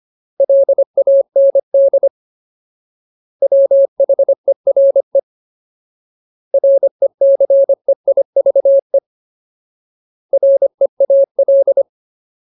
Morse code, heard as a keyed tone, about 25 wpm, 550 Hz. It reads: LAND WHERE RECEIVE REAL
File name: Ultimate Word List / 25WPM Ultimate